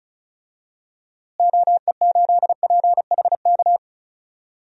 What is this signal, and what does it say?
Morse code, heard as a keyed tone, about 36 wpm, 700 Hz. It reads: OE8PHK
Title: DXCC-100 Call Signs - 6 Characters - Repeated 3x - 35wpm